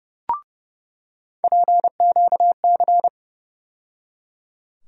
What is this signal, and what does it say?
Morse code, heard as a keyed tone, about 31 wpm, 700 Hz. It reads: PQC
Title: USA Call Sign Postfixes - 3 Letters 30wpm